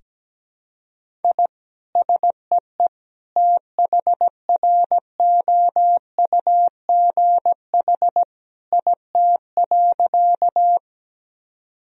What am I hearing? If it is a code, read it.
Morse code, heard as a keyed tone, about 17 wpm, 700 Hz. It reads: I SEE THROUGH IT.
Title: Sentences from Top 1000 Words - Review Part 1 - 17wpm